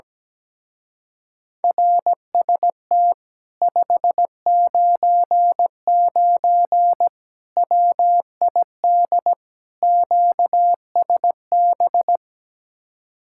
Morse code, heard as a keyed tone, about 17 wpm, 700 Hz. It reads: RST 599 WID QSB
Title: Speed-Racing Common QSO Phrases - 17wpm